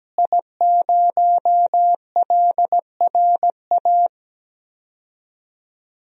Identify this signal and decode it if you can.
Morse code, heard as a keyed tone, about 17 wpm, 700 Hz. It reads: I0LRA
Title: DXCC-100 Call Signs - 5 Characters - Repeated 3x - 17wpm